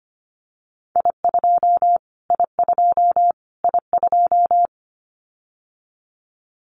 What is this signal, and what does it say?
Morse code, heard as a keyed tone, about 25 wpm, 700 Hz. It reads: I2 I2 I2
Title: Callsign Prefixes for Top 25 DXCC - Repeated 3x - 25wpm